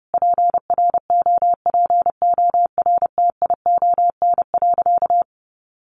Morse code, heard as a keyed tone, about 30 wpm, 700 Hz. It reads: PROPORTION.
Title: A Sound Of Thunder 30wpm